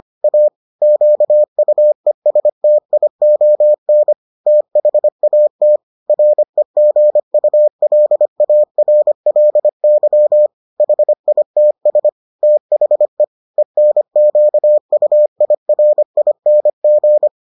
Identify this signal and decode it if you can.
Morse code, heard as a keyed tone, about 25 wpm, 600 Hz. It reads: A QUESTION THAT REGULARLY HITS THE ENQUIRING